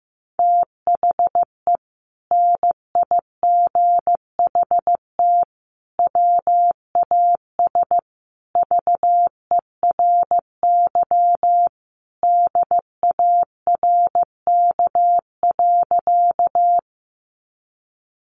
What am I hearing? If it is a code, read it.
Morse code, heard as a keyed tone, about 15 wpm, 700 Hz. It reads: THE NIGHT WAS VERY DARK.
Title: Sentences from Top 500 Words 15wpm